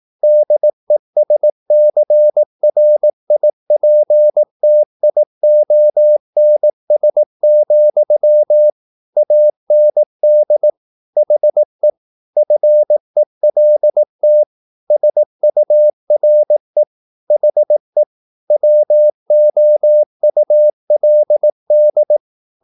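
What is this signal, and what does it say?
Morse code, heard as a keyed tone, about 18 wpm, 600 Hz. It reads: DESCRIPTIONS, AND HE FELT SURE HE WOULD
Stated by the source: Morse practice at 18 wpm, English text, from CW med Gnister.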